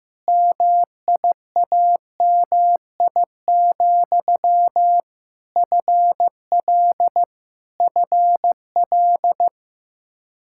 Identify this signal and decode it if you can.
Morse code, heard as a keyed tone, about 15 wpm, 700 Hz. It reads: MIAMI, FL FL
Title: K1USN SST QSO Practice 15wpm